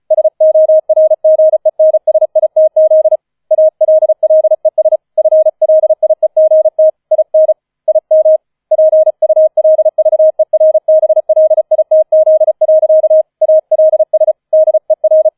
Morse code, heard as a keyed tone, about 34 wpm, 630 Hz. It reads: SORGENSITZ ALLES FLIEGT IN IM PULVERBLITZ. ALS DER
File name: 004 Vierter Streich 35wpm